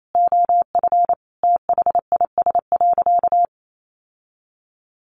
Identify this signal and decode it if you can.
Morse code, heard as a keyed tone, about 28 wpm, 700 Hz. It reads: OF THIS.
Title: Sentences from Top 1100 Words 28wpm